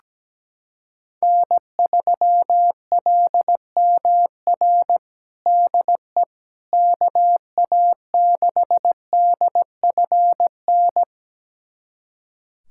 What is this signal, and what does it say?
Morse code, heard as a keyed tone, about 17 wpm, 700 Hz. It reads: N3LMR DE KA6DFN
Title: USA Callsign DE Callsign 17wpm